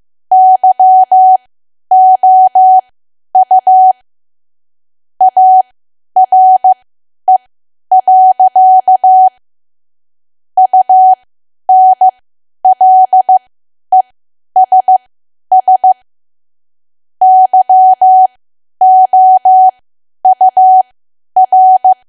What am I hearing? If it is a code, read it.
Morse code, heard as a keyed tone, about 15 wpm, spaced as for 10 wpm, 750 Hz. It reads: YOU ARE. UNLESS YOUR
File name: Archive / 10 WPM / 10WPM